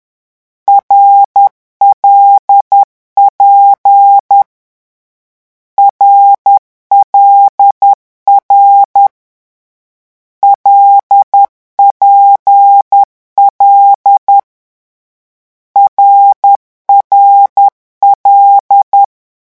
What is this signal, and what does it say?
Morse code, heard as a keyed tone, about 10 wpm, 800 Hz. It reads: RLP RLR LPL RRL
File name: morse mp3 / lecon6-vitesse 10